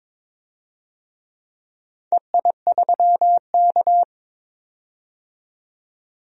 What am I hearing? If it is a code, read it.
Morse code, heard as a keyed tone, about 22 wpm, 700 Hz. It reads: EI3K
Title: DXCC-100 Call Signs - 4 Characters - 22wpm